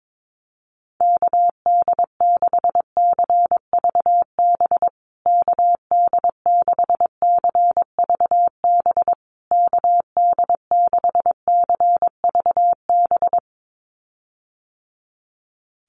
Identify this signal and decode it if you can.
Morse code, heard as a keyed tone, about 22 wpm, 700 Hz. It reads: KD6CVB KD6CVB KD6CVB
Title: USA Call Signs Repeated 3x 22wpm